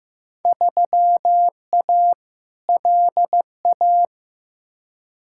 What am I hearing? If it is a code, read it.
Morse code, heard as a keyed tone, about 16 wpm, 700 Hz. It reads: 3A LA
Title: Field Day Exchanges - 15wpm